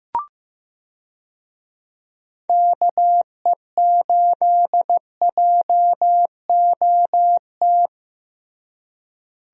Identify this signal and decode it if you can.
Morse code, heard as a keyed tone, about 15 wpm, 700 Hz. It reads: KE8JOT
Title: USA Call Signs - 6 Characters - Repeated 3x - 15wpm